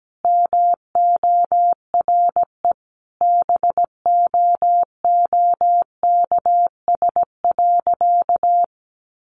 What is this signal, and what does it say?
Morse code, heard as a keyed tone, about 17 wpm, 700 Hz. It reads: MORE BOOKS.
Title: Sentences from Top 1200 Words 17wpm